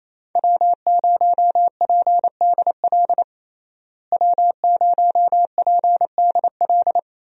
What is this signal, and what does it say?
Morse code, heard as a keyed tone, about 28 wpm, 700 Hz. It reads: W0PDL W0PDL
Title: USA Call Signs - 5 Characters - Repeated 3x - 28wpm